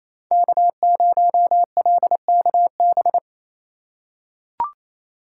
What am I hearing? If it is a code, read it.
Morse code, heard as a keyed tone, about 28 wpm, 700 Hz. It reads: K0LKB
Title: USA Call Signs - 5 Characters - 2x Character Spacing - 28wpm